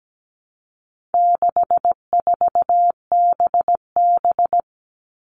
Morse code, heard as a keyed tone, about 17 wpm, 700 Hz. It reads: 64BB
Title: All Cut Numbers - 4-Digit 17wpm